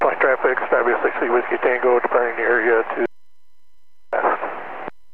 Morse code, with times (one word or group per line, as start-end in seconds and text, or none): none